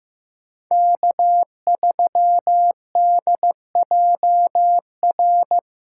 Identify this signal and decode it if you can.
Morse code, heard as a keyed tone, about 15 wpm, 700 Hz. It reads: K3DJR
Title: USA Callsign DE Callsign - 5 Characters - 15wpm